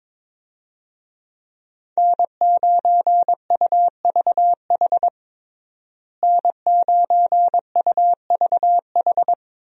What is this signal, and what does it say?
Morse code, heard as a keyed tone, about 22 wpm, 700 Hz. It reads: N9UVH N9UVH
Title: USA Call Signs - 5 Characters - Repeated 3x - 22wpm